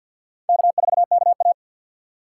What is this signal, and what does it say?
Morse code, heard as a keyed tone, about 50 wpm, 700 Hz. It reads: K4KA
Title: USA Call Signs - 2x Character Spacing - 50wpm